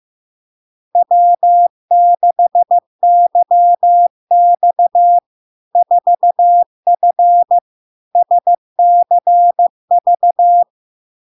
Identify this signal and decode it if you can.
Morse code, heard as a keyed tone, about 15 wpm, 700 Hz. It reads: W6YX 4F SCV
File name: LARK-FD-2022-Log-in-CW-15wpm-700-Hz